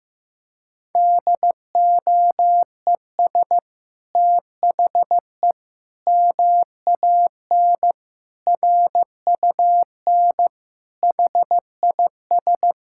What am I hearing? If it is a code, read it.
Morse code, heard as a keyed tone, about 15 wpm, 700 Hz. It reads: DOES THE MAN RUN HIS
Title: Sentences from Top 1000 Words - Review Part 5 - 15wpm